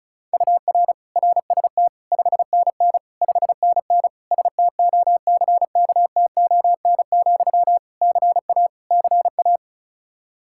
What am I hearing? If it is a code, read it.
Morse code, heard as a keyed tone, about 34 wpm, 700 Hz. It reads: UR RST 5NN 5NN STOCKTON, CA CA
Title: Practice QSO - Exchange 1 - Beginning - 35wpm